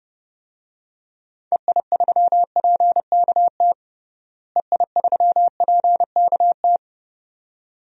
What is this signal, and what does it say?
Morse code, heard as a keyed tone, about 30 wpm, 700 Hz. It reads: EI3PKT EI3PKT
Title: DXCC-100 Call Signs - 6 Characters - Repeated 2x - 30wpm